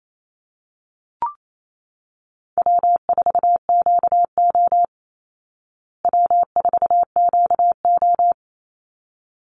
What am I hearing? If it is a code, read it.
Morse code, heard as a keyed tone, about 28 wpm, 700 Hz. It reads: W4QO W4QO
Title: Easy Call Signs Repeated 2x - 28wpm